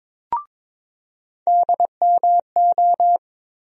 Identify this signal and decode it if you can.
Morse code, heard as a keyed tone, about 22 wpm, 700 Hz. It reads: DMO
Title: USA Call Sign Postfixes - 3 Letters 22wpm